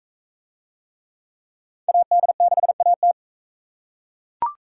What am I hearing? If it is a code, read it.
Morse code, heard as a keyed tone, about 41 wpm, 700 Hz. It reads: AD6AT